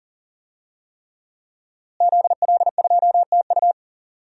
Morse code, heard as a keyed tone, about 40 wpm, 700 Hz. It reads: ZL2TU
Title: DXCC-100 Call Signs - 5 Characters - Repeated 3x - 40wpm